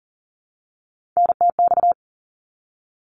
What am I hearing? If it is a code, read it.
Morse code, heard as a keyed tone, about 38 wpm, 700 Hz. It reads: NTX
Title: ARRL Sections 40wpm